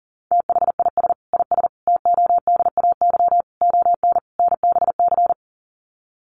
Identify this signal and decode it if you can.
Morse code, heard as a keyed tone, about 40 wpm, 700 Hz. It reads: THIS IS TODAY ON NBC
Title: Sets of 5 Words Encore 40wpm